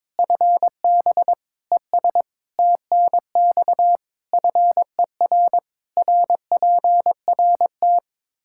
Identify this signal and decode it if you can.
Morse code, heard as a keyed tone, about 21 wpm, 700 Hz. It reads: FB ES TNX FER RPRT
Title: Practice QSO - Exchange 2 - Info - 22wpm